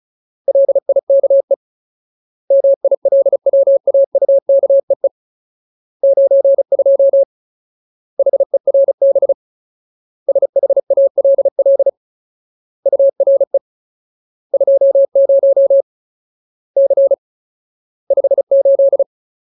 Morse code, heard as a keyed tone, about 34 wpm, 550 Hz. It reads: LIKE MILWAUKEE 92 HERB SHALL URE 20 C 58
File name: Ultimate Word List / 35WPM Ultimate